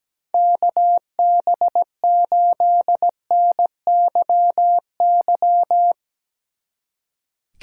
Morse code, heard as a keyed tone, about 17 wpm, 700 Hz. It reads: KB8NYY
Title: Speed Racing - USA 2x3 Call Signs 17wpm